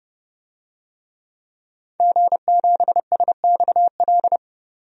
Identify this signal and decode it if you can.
Morse code, heard as a keyed tone, about 29 wpm, 700 Hz. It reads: G7SXL
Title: DXCC-100 Call Signs - 5 Characters - 30wpm